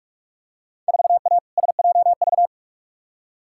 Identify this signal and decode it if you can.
Morse code, heard as a keyed tone, about 46 wpm, 700 Hz. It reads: 4A SJV
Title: Field Day Exchanges Encore - 45wpm